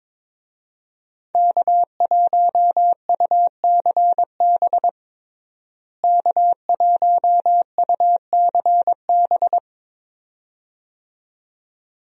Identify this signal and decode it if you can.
Morse code, heard as a keyed tone, about 22 wpm, 700 Hz. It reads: K1UCB K1UCB
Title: USA Call Signs - 5 Characters - Repeated 2x - 22wpm